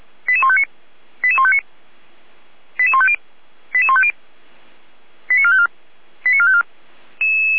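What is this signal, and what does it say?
Five-tone selective call (ZVEI), heard as quick runs of five tones, tones of about 70 ms. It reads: 88159 88159 88150 88150 89454 89454
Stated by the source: ZVEI five-tone call received on SDR.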